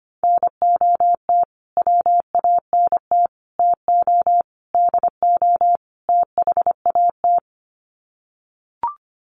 Morse code, heard as a keyed tone, about 25 wpm, 700 Hz. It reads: NOT WANT TO DO THAT
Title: Sets of 5 Words Encore 25wpm